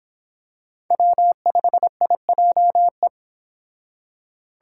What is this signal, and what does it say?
Morse code, heard as a keyed tone, about 26 wpm, 700 Hz. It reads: W5IJE